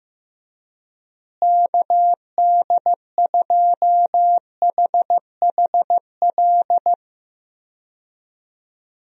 Morse code, heard as a keyed tone, about 16 wpm, 700 Hz. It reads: KD2HHL
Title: USA Call Signs - 6 Characters - Repeated 3x - 15wpm